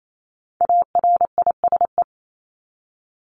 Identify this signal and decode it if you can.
Morse code, heard as a keyed tone, about 28 wpm, 700 Hz. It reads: ARISE
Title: Top 1501-1750 Words 28wpm